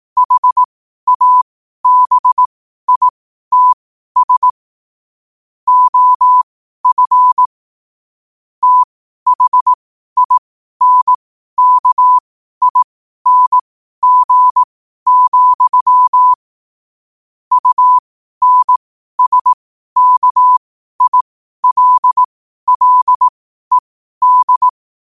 Morse code, heard as a keyed tone, about 18 wpm, spaced as for 12 wpm, 1000 Hz. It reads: HABITS OF THINKING, UNSKILLED
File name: 12wpm